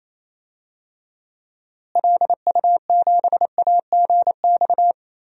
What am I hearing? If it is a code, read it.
Morse code, heard as a keyed tone, about 28 wpm, 700 Hz. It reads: LU7AGX
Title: DXCC-100 Call Signs - 6 Characters - Repeated 3x - 28wpm